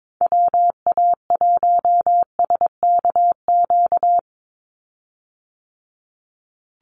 Morse code, heard as a keyed tone, about 22 wpm, 700 Hz. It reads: WA1SKQ
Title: Call Signs Repeated 2x - 22wpm